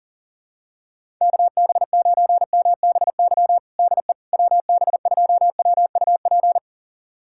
Morse code, heard as a keyed tone, about 37 wpm, 700 Hz. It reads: KB9MBY DE WB2WUP
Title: USA Callsign DE Callsign - 6 Characters - 40wpm